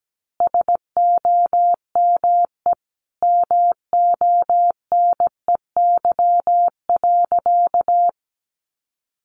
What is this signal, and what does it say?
Morse code, heard as a keyed tone, about 17 wpm, 700 Hz. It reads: SOME MONEY.
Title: Sentences from Top 1500 Words 17wpm